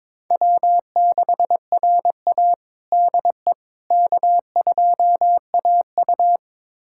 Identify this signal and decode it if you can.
Morse code, heard as a keyed tone, about 22 wpm, 700 Hz. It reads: W6RA DE K2AU
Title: USA Callsign DE Callsign - 4 Characters - 22wpm